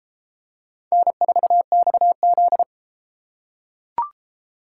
Morse code, heard as a keyed tone, about 32 wpm, 700 Hz. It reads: N4XZ